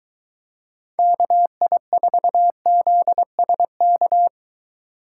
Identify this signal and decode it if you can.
Morse code, heard as a keyed tone, about 21 wpm, 700 Hz. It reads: KI4ZSK